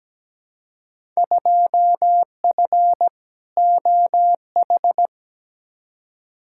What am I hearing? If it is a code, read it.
Morse code, heard as a keyed tone, about 17 wpm, 700 Hz. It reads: 2F OH
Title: Field Day Exchanges - 17wpm